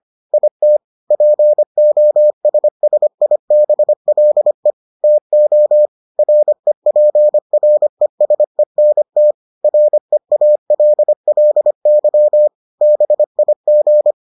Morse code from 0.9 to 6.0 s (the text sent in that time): POSSIBLE TO